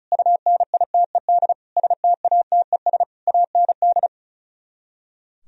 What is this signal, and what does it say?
Morse code, heard as a keyed tone, about 43 wpm, 700 Hz. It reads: UNITED STATES AND
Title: Speed-Racing - Sets of 5 Words 35wpm